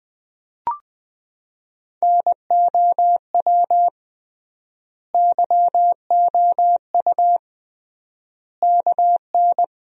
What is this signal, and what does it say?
Morse code, heard as a keyed tone, about 20 wpm, 700 Hz. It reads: NOW YOU KN
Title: 3x Word-Spacing - Sentences from Top 100 Words - 20wpm